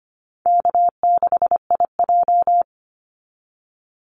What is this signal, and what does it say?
Morse code, heard as a keyed tone, about 25 wpm, 700 Hz. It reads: K6IJ
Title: Easy Call Signs - 25wpm